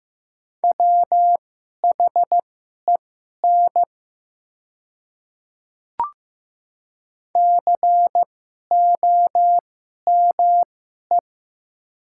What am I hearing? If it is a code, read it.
Morse code, heard as a keyed tone, about 15 wpm, spaced as for 11 wpm, 700 Hz. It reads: WHEN COME
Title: 2x Character Spacing - Top 100 Words - 15wpm